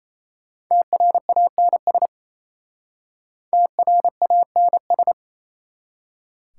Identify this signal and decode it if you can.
Morse code, heard as a keyed tone, about 32 wpm, 700 Hz. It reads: TRANS TRANS
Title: Speed-Racing Prefixes and Suffixes 28wpm